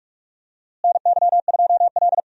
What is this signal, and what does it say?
Morse code, heard as a keyed tone, about 42 wpm, 700 Hz. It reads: NY2L
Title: USA Call Signs - 4 Characters - Repeated 3x - 45wpm